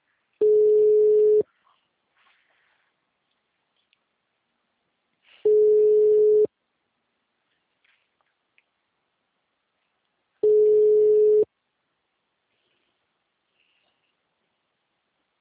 Ringback tone (Europe, 425 Hz).